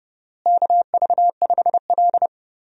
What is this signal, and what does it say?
Morse code, heard as a keyed tone, about 30 wpm, 700 Hz. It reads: KV5L
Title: USA Call Signs - 4 Characters - Repeated 3x - 30wpm